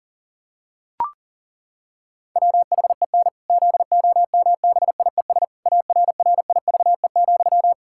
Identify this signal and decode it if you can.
Morse code, heard as a keyed tone, about 40 wpm, 700 Hz. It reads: WHEN ZOMBIES ARRIVE,
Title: Pangram Challenge v2 - 40wpm